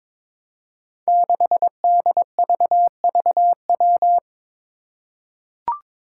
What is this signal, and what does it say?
Morse code, heard as a keyed tone, about 22 wpm, 700 Hz. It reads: 6DVVW
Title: Five Challenging Letter-Number Combinations 22wpm